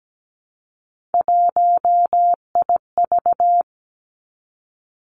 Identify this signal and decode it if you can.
Morse code, heard as a keyed tone, about 17 wpm, 700 Hz. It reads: 1IV
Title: Three Challenging Letter-Number Combinations 17wpm